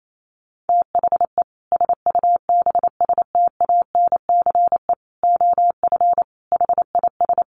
Morse code, heard as a keyed tone, about 28 wpm, 700 Hz. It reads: THE SUBSTANCE OF HIS
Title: Sentences from Top 1500 Words 28wpm